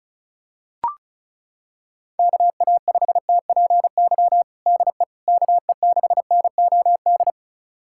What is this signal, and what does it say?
Morse code, heard as a keyed tone, about 34 wpm, 700 Hz. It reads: KA5TPY DE KE6NOD
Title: USA Callsign DE Callsign - 6 Characters - 35wpm